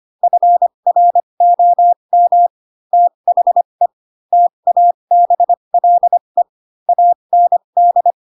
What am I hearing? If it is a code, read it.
Morse code, heard as a keyed tone, about 25 wpm, spaced as for 22 wpm, 700 Hz. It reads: FROM THE TABLE AND